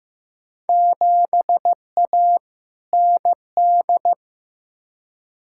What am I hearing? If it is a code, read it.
Morse code, heard as a keyed tone, about 15 wpm, 700 Hz. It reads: 7A ND
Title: Field Day Exchanges Encore - 15wpm